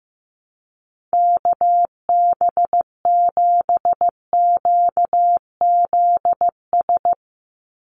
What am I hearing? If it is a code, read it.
Morse code, heard as a keyed tone, about 15 wpm, 700 Hz. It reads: KB7QZS
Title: USA Call Signs 15wpm